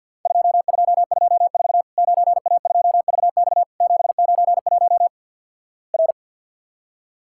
Morse code, heard as a keyed tone, about 50 wpm, 700 Hz. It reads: 2224 9A2VX 791 R
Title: Worked All Europe DX Contest - QTC Exchange - 50wpm